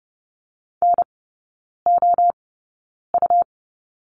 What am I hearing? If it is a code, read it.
Morse code, heard as a keyed tone, about 30 wpm, spaced as for 9 wpm, 690 Hz. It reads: NOU
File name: Lesson 17e - Letter u - Three Characters 30wpm